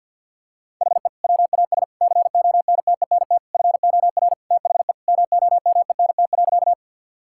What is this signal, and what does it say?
Morse code, heard as a keyed tone, about 46 wpm, 700 Hz. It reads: HE WAS CONTENT FOR THE MOMENT.
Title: Sentences from Top 1000 Words - Review Part 6 - 50wpm